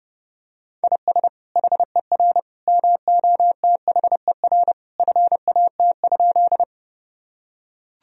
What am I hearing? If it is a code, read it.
Morse code, heard as a keyed tone, about 32 wpm, 700 Hz. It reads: IS HER MOTHER FAT?
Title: Sentences from Top 1000 Words - Review Part 4 - 30wpm